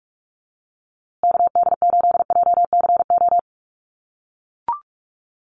Morse code, heard as a keyed tone, about 44 wpm, 690 Hz. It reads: KD8JCO
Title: USA Call Signs Repeated 3x 45wpm